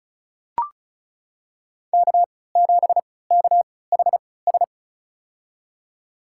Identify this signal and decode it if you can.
Morse code, heard as a keyed tone, about 36 wpm, spaced as for 20 wpm, 700 Hz. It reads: K7KHS
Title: USA Call Signs - 5 Characters - 3x Character Spacing - 35wpm